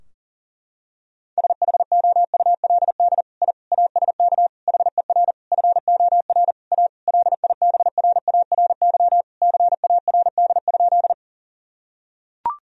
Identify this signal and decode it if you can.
Morse code, heard as a keyed tone, about 38 wpm, 700 Hz. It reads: SHOULD I ASK HER FOR A LIBRARY CARD?
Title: Sentences from Top 1000 Words - Review Part 4 - 40wpm